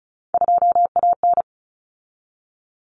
Morse code, heard as a keyed tone, about 34 wpm, 700 Hz. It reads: 2AN
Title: Common Cut Numbers - 3-Digit 35wpm